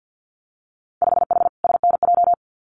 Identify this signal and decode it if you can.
Morse code, heard as a keyed tone, about 50 wpm, 700 Hz. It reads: 5H SNJ
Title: Winter Field Day Exchanges Encore 50wpm